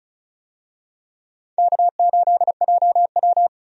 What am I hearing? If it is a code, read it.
Morse code, heard as a keyed tone, about 34 wpm, 700 Hz. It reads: K8JW
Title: USA Call Signs - 4 Characters - 35wpm